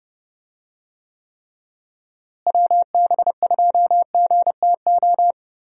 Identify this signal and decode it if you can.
Morse code, heard as a keyed tone, about 30 wpm, 700 Hz. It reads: WB2GTO
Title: USA Call Signs - 6 Characters - Repeated 3x - 30wpm